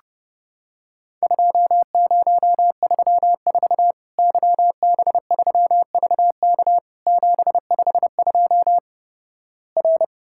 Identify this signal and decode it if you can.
Morse code, heard as a keyed tone, about 30 wpm, 700 Hz. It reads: 2034 YB3VK 752 R
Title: Worked All Europe DX Contest - QTC Exchange - 30wpm